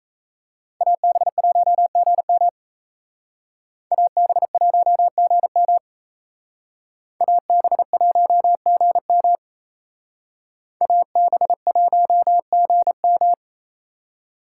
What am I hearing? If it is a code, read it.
Morse code, heard as a keyed tone, about 34 wpm, 700 Hz. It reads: AB1GM AB1GM AB1GM AB1GM
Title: Speed Racing - USA 2x2 Call Signs 28wpm